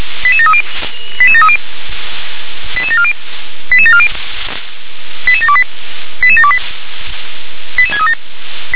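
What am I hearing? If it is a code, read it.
Five-tone selective call (ZVEI), heard as quick runs of five tones, tones of about 70 ms. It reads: 88730 88730 88740 88740 88728 88728 88637
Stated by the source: ZVEI five-tone call 88913 received with an SDR.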